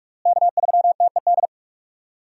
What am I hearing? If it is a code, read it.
Morse code, heard as a keyed tone, about 46 wpm, 700 Hz. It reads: K3TED
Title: USA Call Signs - 5 Characters - Repeated 2x - 45wpm